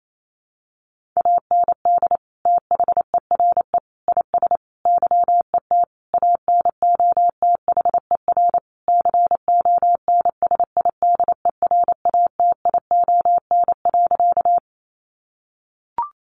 Morse code, heard as a keyed tone, about 28 wpm, 700 Hz. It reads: AND THERE IS YET ANOTHER CONSIDERATION.
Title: Sentences from Top 1500 Words 28wpm